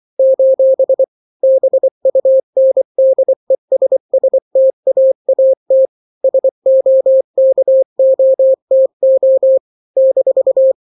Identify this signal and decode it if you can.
Morse code, heard as a keyed tone, about 24 wpm, 530 Hz. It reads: : BUNDESSTAAT SOKOTO -
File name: daily / 23wpm